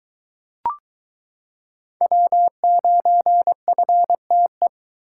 Morse code, heard as a keyed tone, about 22 wpm, 700 Hz. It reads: W9FTE